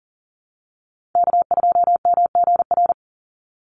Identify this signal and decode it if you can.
Morse code, heard as a keyed tone, about 40 wpm, 700 Hz. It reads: K2MGR
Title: Call Signs Repeated 3x - 40wpm